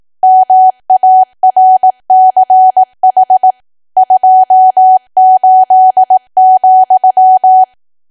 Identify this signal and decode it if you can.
Morse code, heard as a keyed tone, about 18 wpm, 750 Hz. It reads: MARCH 28,
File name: cwa / W1AW-Level2 / 18WPM